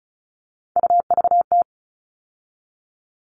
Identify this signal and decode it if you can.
Morse code, heard as a keyed tone, about 34 wpm, 700 Hz. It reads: UVT
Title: All Cut Numbers - 3-Digit 35wpm